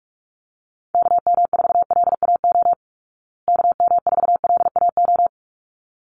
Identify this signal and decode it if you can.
Morse code, heard as a keyed tone, about 44 wpm, 700 Hz. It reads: KM4LAO KM4LAO
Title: Call Signs Repeated 3x - 45wpm